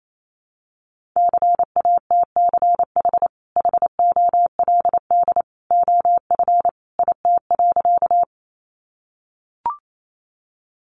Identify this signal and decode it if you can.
Morse code, heard as a keyed tone, about 28 wpm, 700 Hz. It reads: CATCH HOLD OF IT.
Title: Sentences from Top 1500 Words 28wpm